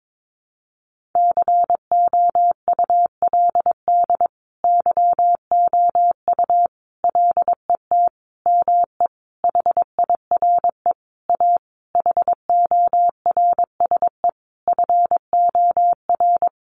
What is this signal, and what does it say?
Morse code, heard as a keyed tone, about 22 wpm, 700 Hz. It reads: COULD YOU LET ME HIRE A HORSE FOR
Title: Sentences from Top 1400 Words 22wpm